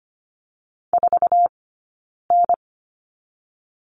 Morse code, heard as a keyed tone, about 25 wpm, spaced as for 9 wpm, 700 Hz. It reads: 4N
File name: Lesson 15d - Number 5 - Two Characters 25wpm